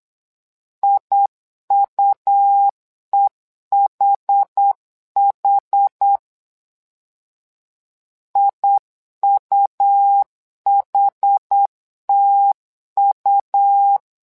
Morse code, heard as a keyed tone, about 8 wpm, 800 Hz. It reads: IUEHH IUHTU
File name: LEZIONI / 2 UHDN / 2 UHD 8wpm 2258